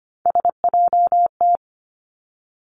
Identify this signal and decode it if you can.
Morse code, heard as a keyed tone, about 25 wpm, 700 Hz. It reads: SJT
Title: Three Challenging Letter-Number Combinations 25wpm